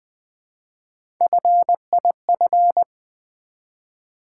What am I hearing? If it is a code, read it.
Morse code, heard as a keyed tone, about 20 wpm, 700 Hz. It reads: FIF
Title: Build A Word - 1 Letter At A Time - 5-Letter Words - 20wpm